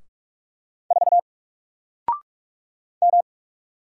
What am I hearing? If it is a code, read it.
Morse code, heard as a keyed tone, about 44 wpm, 700 Hz. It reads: 4 M
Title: Single Character (Letter:Number:Punctuation:Prosign) 45wpm